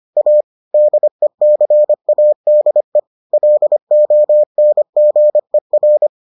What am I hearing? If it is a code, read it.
Morse code, heard as a keyed tone, about 25 wpm, 600 Hz. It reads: A DECADE LONGER